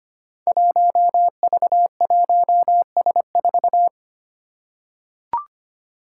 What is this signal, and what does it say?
Morse code, heard as a keyed tone, about 25 wpm, 700 Hz. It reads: 1V1S4
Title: Five Challenging Letter-Number Combinations 25wpm